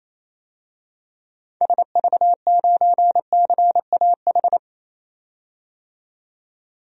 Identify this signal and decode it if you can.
Morse code, heard as a keyed tone, about 28 wpm, 700 Hz. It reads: SV9CAH
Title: DXCC-100 Call Signs - 6 Characters - 28wpm